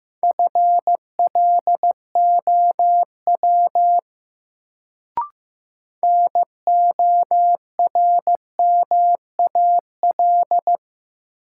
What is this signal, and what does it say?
Morse code, heard as a keyed tone, about 15 wpm, 700 Hz. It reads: FLOW NORMAL
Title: Sentences from Top 1200 Words - New Word Drill Alt - 15wpm